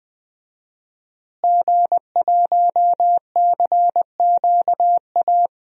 Morse code, heard as a keyed tone, about 20 wpm, 700 Hz. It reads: G1CQA
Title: DXCC-100 Call Signs - 5 Characters - Repeated 3x - 20wpm